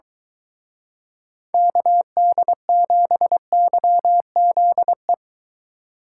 Morse code, heard as a keyed tone, about 23 wpm, 700 Hz. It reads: KD7YZE